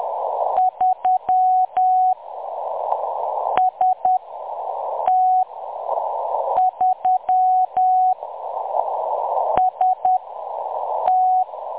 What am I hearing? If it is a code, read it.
Morse code, heard as a keyed tone, about 10 wpm, 740 Hz. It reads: 3 S T 3 S T
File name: M23-ST3- -0858utc-10184khz